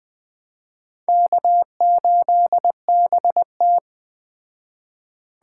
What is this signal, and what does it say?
Morse code, heard as a keyed tone, about 20 wpm, 700 Hz. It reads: K8BT
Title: Speed Racing - USA 1x2 Call Signs 17wpm